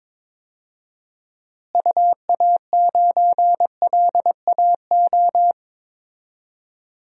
Morse code, heard as a keyed tone, about 21 wpm, 700 Hz. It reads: UA9LAO
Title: DXCC-100 Call Signs - 6 Characters - Repeated 2x - 22wpm